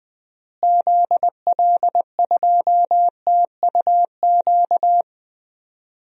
Morse code, heard as a keyed tone, about 21 wpm, 700 Hz. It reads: ZL2TUQ
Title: DXCC-100 Call Signs - 6 Characters - Repeated 2x - 20wpm